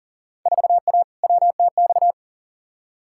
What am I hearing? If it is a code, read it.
Morse code, heard as a keyed tone, about 40 wpm, 700 Hz. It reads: 4A WTX
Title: Field Day Exchanges - 40wpm